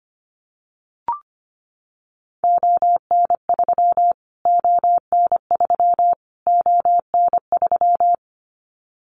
Morse code, heard as a keyed tone, about 25 wpm, 700 Hz. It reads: ON3 ON3 ON3
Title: Callsign Prefixes for Top 25 DXCC - Repeated 3x - 25wpm